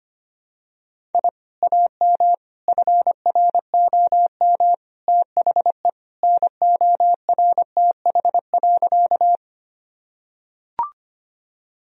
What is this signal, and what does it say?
Morse code, heard as a keyed tone, about 25 wpm, 700 Hz. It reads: I AM FROM THE NORTH.
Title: Sentences from Top 1000 Words - Review Part 1 - 25wpm